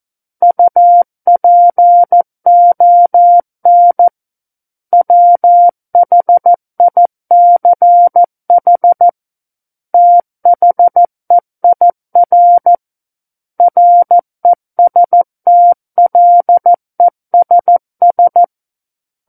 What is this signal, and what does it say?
Morse code, heard as a keyed tone, about 14 wpm, 700 Hz. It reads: UPON WHICH THEIR RESTLESS
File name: Book One / 05-Chapter 5 - 14 WPM